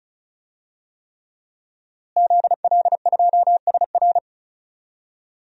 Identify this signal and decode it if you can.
Morse code, heard as a keyed tone, about 34 wpm, 700 Hz. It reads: ZL2SR
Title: DXCC-100 Call Signs - 5 Characters - Repeated 3x - 35wpm